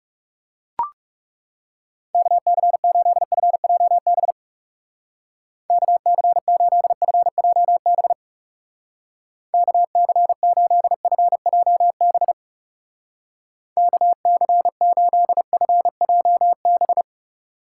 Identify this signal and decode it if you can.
Morse code, heard as a keyed tone, about 37 wpm, 700 Hz. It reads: KC8FJB KC8FJB KC8FJB KC8FJB
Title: Speed Racing - USA 2x3 Call Signs 30wpm